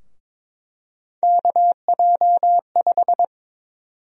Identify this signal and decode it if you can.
Morse code, heard as a keyed tone, about 22 wpm, 700 Hz. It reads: KJ5
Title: USA Call Sign Prefixes - Repeated 2x - 22wpm